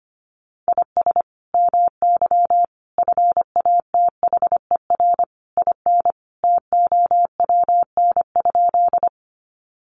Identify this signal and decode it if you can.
Morse code, heard as a keyed tone, about 25 wpm, 700 Hz. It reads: IS MY FATHER IN TOWN?
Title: Sentences from Top 500 Words 25wpm